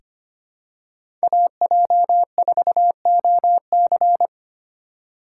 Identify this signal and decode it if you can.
Morse code, heard as a keyed tone, about 25 wpm, 700 Hz. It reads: AJ4OC
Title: USA Call Signs - 5 Characters - 4x Character Spacing - 25wpm